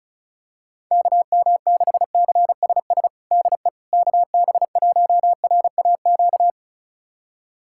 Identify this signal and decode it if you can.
Morse code, heard as a keyed tone, about 36 wpm, 700 Hz. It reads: KM6CSS DE KB1RAQ
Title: USA Callsign DE Callsign 35wpm